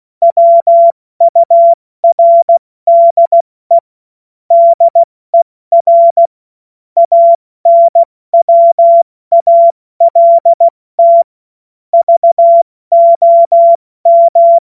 Morse code, heard as a keyed tone, about 16 wpm, 680 Hz. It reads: WURDE DER ANWALT VOM
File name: Morsekurs / Witze / Witz 25. Oktober 2022 70 F80